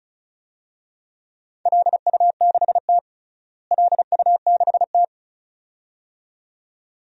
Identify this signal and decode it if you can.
Morse code, heard as a keyed tone, about 34 wpm, 700 Hz. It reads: LU6T LU6T
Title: DXCC-100 Call Signs - 4 Characters - Repeated 2x - 35wpm